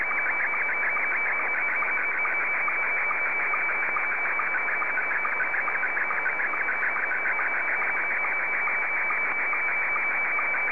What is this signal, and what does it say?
SSTV (slow-scan television), Scottie 1.